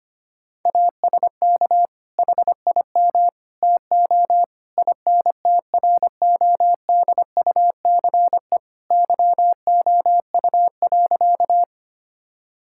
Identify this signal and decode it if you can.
Morse code, heard as a keyed tone, about 25 wpm, 700 Hz. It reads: ASK HIM TO INTRODUCE YOU.
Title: Sentences from Top 1600 Words 25wpm